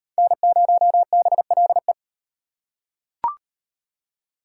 Morse code, heard as a keyed tone, about 35 wpm, 700 Hz. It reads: N0BLE